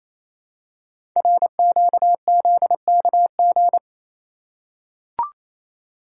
Morse code, heard as a keyed tone, about 28 wpm, 700 Hz. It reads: RQZKG
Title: Code Groups - 5 Letter:Numbers - 28wpm